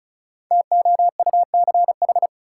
Morse code, heard as a keyed tone, about 36 wpm, 700 Hz. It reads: TOUCH
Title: Sentences from Top 1200 Words - New Word Drill Alt - 35wpm